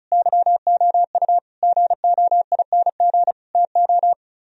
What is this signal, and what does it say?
Morse code, heard as a keyed tone, about 44 wpm, 700 Hz. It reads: YOU GOING TO